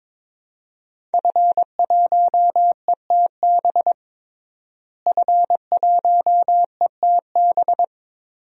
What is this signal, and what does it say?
Morse code, heard as a keyed tone, about 22 wpm, 700 Hz. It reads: F1ETB F1ETB
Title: DXCC-100 Call Signs - 5 Characters - Repeated 3x - 22wpm